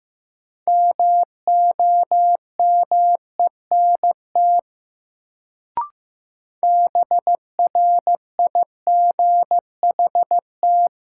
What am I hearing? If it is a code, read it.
Morse code, heard as a keyed tone, about 15 wpm, 700 Hz. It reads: MOMENT BRIGHT
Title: Sentences from Top 1000 Words - New Word Drill - 15wpm